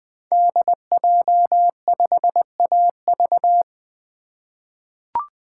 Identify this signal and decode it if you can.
Morse code, heard as a keyed tone, about 21 wpm, 700 Hz. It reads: DJ5AV
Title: 2023 SOTA Chaser Call Signs - 20wpm